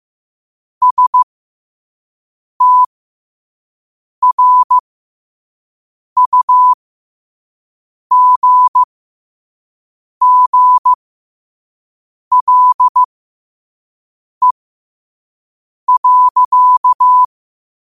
Morse code, heard as a keyed tone, about 15 wpm, spaced as for 5 wpm, 1000 Hz. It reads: STRUGGLE.
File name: qotdfiles / 5wpm